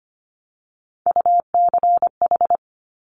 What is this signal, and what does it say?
Morse code, heard as a keyed tone, about 25 wpm, 700 Hz. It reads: UCH
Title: Top Three Letter Combinations in Words 25wpm